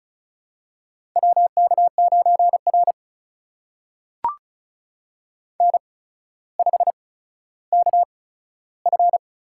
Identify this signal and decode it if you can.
Morse code, heard as a keyed tone, about 35 wpm, spaced as for 15 wpm, 700 Hz. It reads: WK9R N5KF
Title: USA Call Signs - 4 Characters - 8x Character Spacing - 35wpm